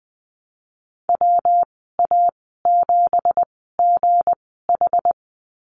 Morse code, heard as a keyed tone, about 20 wpm, spaced as for 14 wpm, 700 Hz. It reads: WA7GH
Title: 219 Lesson 35g - Number 0 - Call Signs 20wpm